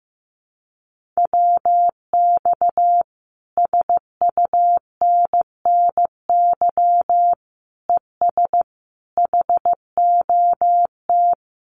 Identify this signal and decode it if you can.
Morse code, heard as a keyed tone, about 15 wpm, 700 Hz. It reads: WX SUNNY ES HOT
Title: Common QSO Phrases Encore 15wpm